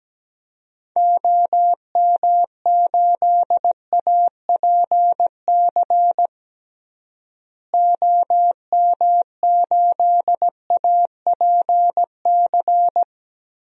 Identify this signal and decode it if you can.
Morse code, heard as a keyed tone, about 17 wpm, 700 Hz. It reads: OM8APC OM8APC
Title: DXCC-100 Call Signs - 6 Characters - Repeated 3x - 17wpm